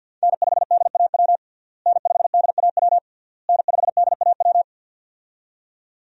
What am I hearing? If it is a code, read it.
Morse code, heard as a keyed tone, about 52 wpm, 700 Hz. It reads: N5DAW N5DAW N5DAW
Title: USA Call Signs - 5 Characters - Repeated 3x - 50wpm